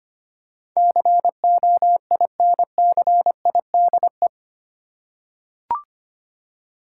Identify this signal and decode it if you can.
Morse code, heard as a keyed tone, about 25 wpm, 700 Hz. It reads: COINCIDE
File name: Eight Letter Words - 25wpm